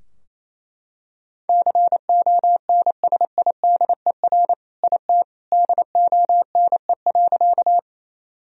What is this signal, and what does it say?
Morse code, heard as a keyed tone, about 27 wpm, 700 Hz. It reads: CONSIDER IT DONE.
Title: Sentences from Top 1000 Words - Review Part 4 - 28wpm